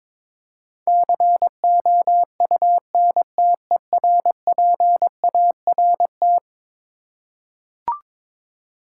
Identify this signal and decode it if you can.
Morse code, heard as a keyed tone, about 22 wpm, 700 Hz. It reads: COUNTERPART
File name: practice / 22wpm / Eleven Letter Words - 22wpm